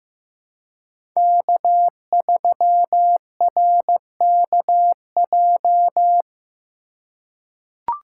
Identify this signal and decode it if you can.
Morse code, heard as a keyed tone, about 15 wpm, 700 Hz. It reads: K3RKJ
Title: USA Call Signs - 5 Characters - 2x Character Spacing - 15wpm